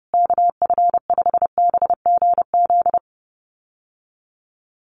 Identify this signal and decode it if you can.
Morse code, heard as a keyed tone, about 30 wpm, 700 Hz. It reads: KF5BGZ
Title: USA Call Signs Repeated 3x 30wpm